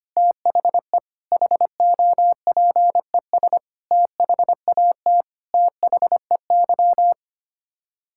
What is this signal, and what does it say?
Morse code, heard as a keyed tone, about 25 wpm, 700 Hz. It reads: THE HOPES THAT THEY
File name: Sets of 9 Words - 25wpm